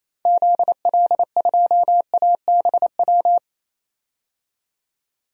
Morse code, heard as a keyed tone, about 28 wpm, 700 Hz. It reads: ZL2ABW
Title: DXCC-100 Call Signs - 6 Characters - 28wpm